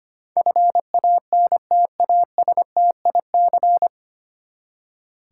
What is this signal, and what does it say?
Morse code, heard as a keyed tone, about 25 wpm, 700 Hz. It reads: FANTASTIC
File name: Nine Letter Words - 25wpm